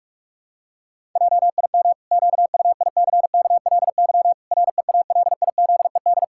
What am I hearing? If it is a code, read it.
Morse code, heard as a keyed tone, about 44 wpm, 700 Hz. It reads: JIM QUICKLY REALIZED
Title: Pangram Challenge v2 - 45wpm